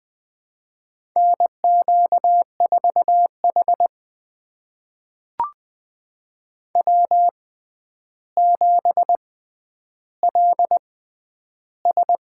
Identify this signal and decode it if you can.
Morse code, heard as a keyed tone, about 20 wpm, spaced as for 10 wpm, 700 Hz. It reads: NQ4H W7LS
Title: USA Call Signs - 4 Characters - 6x Character Spacing - 20wpm